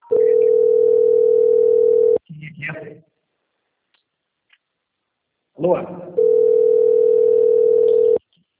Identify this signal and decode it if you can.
Ringback tone (North America), 440 + 480 Hz, with speech.
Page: Capital - SP